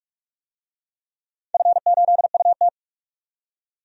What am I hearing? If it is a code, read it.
Morse code, heard as a keyed tone, about 47 wpm, 700 Hz. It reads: F8UT
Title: DXCC-100 Call Signs - 4 Characters - 45wpm